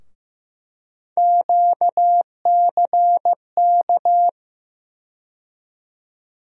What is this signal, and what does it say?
Morse code, heard as a keyed tone, about 15 wpm, 700 Hz. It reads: QCK
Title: USA Call Sign Postfixes - 3 Letters 15wpm